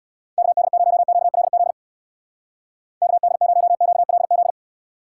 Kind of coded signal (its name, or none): Morse code